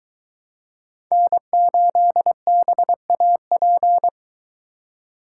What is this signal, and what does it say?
Morse code, heard as a keyed tone, about 22 wpm, 700 Hz. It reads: N8BAP